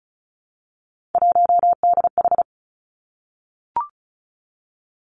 Morse code, heard as a keyed tone, about 36 wpm, 700 Hz. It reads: 1DH
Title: Three Challenging Letter-Number Combinations 35wpm